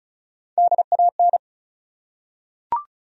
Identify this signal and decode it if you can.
Morse code, heard as a keyed tone, about 35 wpm, 700 Hz. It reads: DAN
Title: USA Common Names - Max 4 Letters - 35wpm